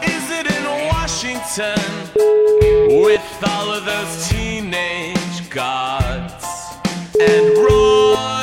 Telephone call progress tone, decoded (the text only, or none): ringback tone